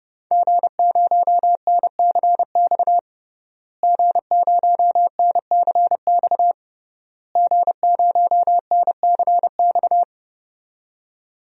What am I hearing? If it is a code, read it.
Morse code, heard as a keyed tone, about 30 wpm, 700 Hz. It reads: G0NCX G0NCX G0NCX
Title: DXCC-100 Call Signs - 5 Characters - Repeated 3x - 30wpm